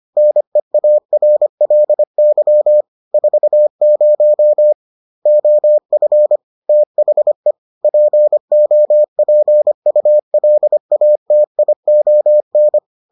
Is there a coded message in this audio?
Yes: Morse code from 0.2 to 12.8 s, heard as a keyed tone, about 25 wpm, 600 Hz. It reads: NEARLY 40 OF THE POPULATION